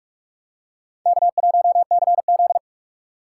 Morse code, heard as a keyed tone, about 46 wpm, 700 Hz. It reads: K1CZ
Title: USA Call Signs - 4 Characters - 6x Character Spacing - 45wpm